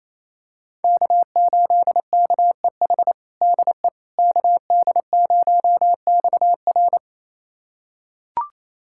Morse code, heard as a keyed tone, about 28 wpm, 700 Hz. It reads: K8KEH DE KD0XR
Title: USA Callsign DE Callsign - 5 Characters - 28wpm